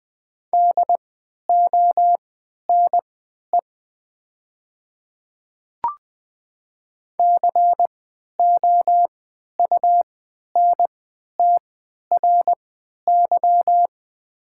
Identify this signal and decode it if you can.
Morse code, heard as a keyed tone, about 20 wpm, spaced as for 11 wpm, 700 Hz. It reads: DONE COUNTRY
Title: 3x Character Spacing - Top 500 Words - 20wpm